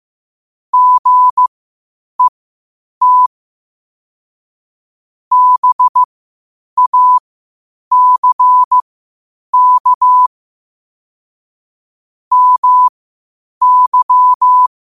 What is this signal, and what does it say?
Morse code, heard as a keyed tone, about 15 wpm, spaced as for 8 wpm, 1000 Hz. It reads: GET BACK MY
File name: 8wpm